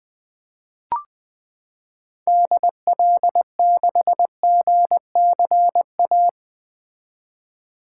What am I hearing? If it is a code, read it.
Morse code, heard as a keyed tone, about 20 wpm, 700 Hz. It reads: DL6GCA
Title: 2023 SOTA Activator Call Signs - 20wpm